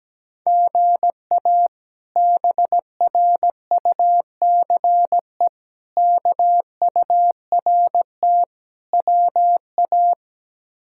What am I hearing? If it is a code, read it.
Morse code, heard as a keyed tone, about 17 wpm, 700 Hz. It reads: GA BRUCE KURT WA
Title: K1USN SST QSO Practice - 17wpm